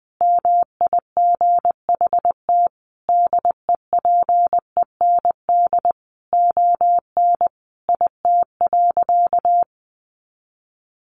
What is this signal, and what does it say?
Morse code, heard as a keyed tone, about 19 wpm, 700 Hz. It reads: MIGHT DEPEND ON IT.
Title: Sentences from Top 1300 Words 20wpm